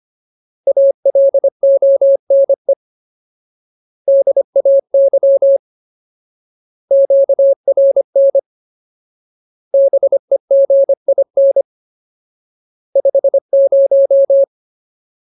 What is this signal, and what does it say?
Morse code, heard as a keyed tone, about 25 wpm, 550 Hz. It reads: ALONE DAY QRN BEGIN 50
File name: Ultimate Word List / 25WPM Ultimate